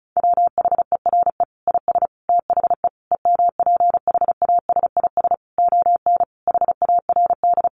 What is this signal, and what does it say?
Morse code, heard as a keyed tone, about 34 wpm, 700 Hz. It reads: WHERE IS THE EMPHASIS ON HARD
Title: Sentences from Top 1400 Words 35wpm